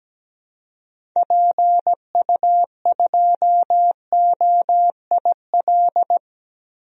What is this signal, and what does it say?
Morse code, heard as a keyed tone, about 17 wpm, 700 Hz. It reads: PU2OIL
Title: DXCC-100 Call Signs - 6 Characters - 17wpm